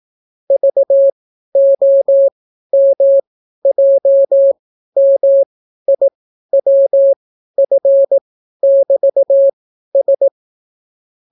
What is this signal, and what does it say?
Morse code, heard as a keyed tone, about 18 wpm, spaced as for 12 wpm, 550 Hz. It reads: VOMJMIWF<BT>S